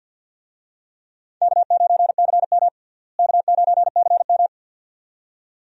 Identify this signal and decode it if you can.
Morse code, heard as a keyed tone, about 45 wpm, 700 Hz. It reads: K9CM K9CM
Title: USA Call Signs - 4 Characters - Repeated 2x - 50wpm